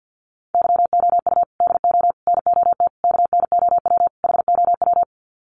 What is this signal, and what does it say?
Morse code, heard as a keyed tone, about 50 wpm, 700 Hz. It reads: YOU DO NOT KNOW HOW
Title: Sets of 5 Words Encore 50wpm